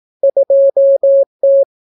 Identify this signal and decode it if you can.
Morse code, heard as a keyed tone, about 18 wpm, 550 Hz. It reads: 2T